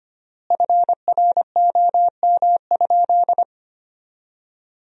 Morse code, heard as a keyed tone, about 25 wpm, 700 Hz. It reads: FROM?
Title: 3x Word-Spacing - Sentences from Top 200 Words - 25wpm